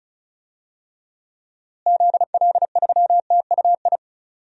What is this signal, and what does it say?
Morse code, heard as a keyed tone, about 36 wpm, 700 Hz. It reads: ZL3TUI
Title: DXCC-100 Call Signs - 6 Characters - Repeated 2x - 35wpm